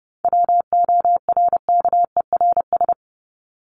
Morse code, heard as a keyed tone, about 30 wpm, 700 Hz. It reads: WORKERS
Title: ES:S Suffix - Top 100 Words with Base Words 30wpm